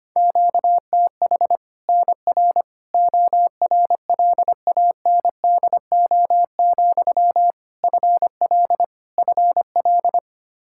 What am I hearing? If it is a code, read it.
Morse code, heard as a keyed tone, about 25 wpm, 700 Hz. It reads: QTH NR ORLANDO, FL FL
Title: Practice QSO - Exchange 1 - Beginning - 25wpm